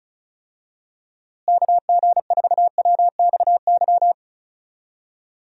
Morse code, heard as a keyed tone, about 36 wpm, 700 Hz. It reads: KG4WXY
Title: USA Call Signs - 6 Characters - Repeated 3x - 35wpm